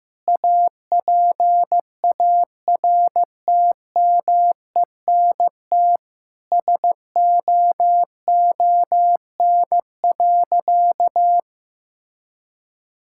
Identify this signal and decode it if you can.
Morse code, heard as a keyed tone, about 15 wpm, 700 Hz. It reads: APARTMENT SOON.
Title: Sentences from Top 1000 Words - Review Part 6 - 15wpm